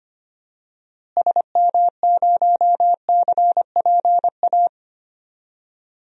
Morse code, heard as a keyed tone, about 25 wpm, 700 Hz. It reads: SM0CPA
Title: DXCC-100 Call Signs - 6 Characters - Repeated 3x - 25wpm